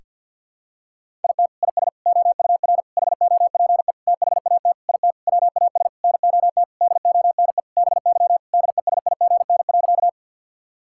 Morse code, heard as a keyed tone, about 53 wpm, 700 Hz. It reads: IT IS OUR HOPE THAT IT WAS NOT DONE BY DESIGN.
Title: Sentences from Top 1000 Words - Review Part 5 - 50wpm